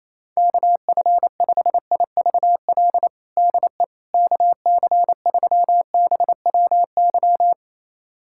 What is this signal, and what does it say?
Morse code, heard as a keyed tone, about 28 wpm, 700 Hz. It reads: KF5IVL DE KC3BWY
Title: USA Callsign DE Callsign - 6 Characters - 28wpm